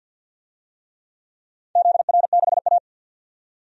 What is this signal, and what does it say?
Morse code, heard as a keyed tone, about 47 wpm, 700 Hz. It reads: ZR6A
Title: DXCC-100 Call Signs - 4 Characters - 50wpm